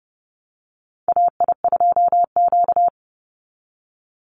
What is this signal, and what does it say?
Morse code, heard as a keyed tone, about 30 wpm, 700 Hz. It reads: AI2Q
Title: Call Signs Repeated 3x - 30wpm